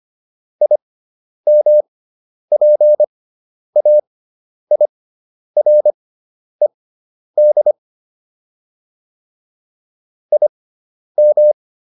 Morse code, heard as a keyed tone, about 25 wpm, spaced as for 10 wpm, 600 Hz. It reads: IMPAIRED IM